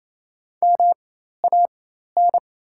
Morse code, heard as a keyed tone, about 28 wpm, spaced as for 13 wpm, 690 Hz. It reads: MAN
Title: 4x Character Spacing - Top 300 Words - 28wpm